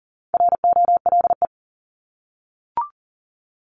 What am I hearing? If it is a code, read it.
Morse code, heard as a keyed tone, about 40 wpm, 700 Hz. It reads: ROLE
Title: Top 751-1000 Words 40wpm